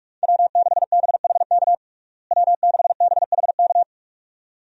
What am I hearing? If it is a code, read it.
Morse code, heard as a keyed tone, about 46 wpm, 700 Hz. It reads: W6BHK W6BHK
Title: USA Call Signs - 5 Characters - Repeated 2x - 45wpm